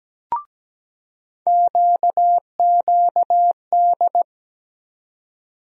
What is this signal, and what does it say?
Morse code, heard as a keyed tone, about 17 wpm, 700 Hz. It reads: QQD
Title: USA Call Sign Postfixes - 3 Letters 17wpm